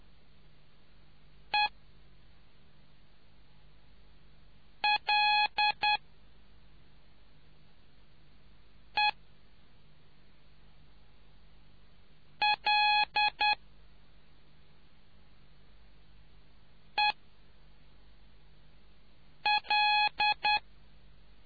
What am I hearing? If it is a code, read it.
Morse code, heard as a keyed tone, about 10 wpm, 840 Hz. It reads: E L E L E L